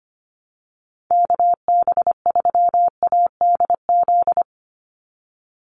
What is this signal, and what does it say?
Morse code, heard as a keyed tone, about 25 wpm, 700 Hz. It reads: KB3ADZ
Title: USA Call Signs 25wpm